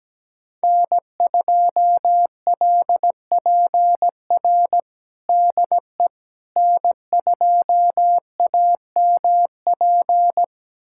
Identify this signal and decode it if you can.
Morse code, heard as a keyed tone, about 17 wpm, 700 Hz. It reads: N2LPR DE N2AMP
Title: USA Callsign DE Callsign - 5 Characters - 17wpm